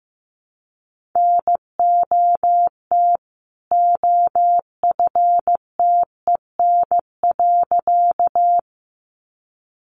Morse code, heard as a keyed tone, about 15 wpm, 700 Hz. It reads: NOT OFTEN.
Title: A Sound Of Thunder 15wpm